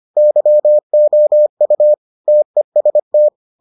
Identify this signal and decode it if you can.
Morse code, heard as a keyed tone, about 25 wpm, 600 Hz. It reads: YOU TEST